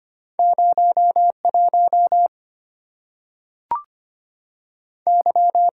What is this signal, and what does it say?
Morse code, heard as a keyed tone, about 25 wpm, 700 Hz. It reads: 0 1 Y
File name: Two Letter-Number Combinations - 25wpm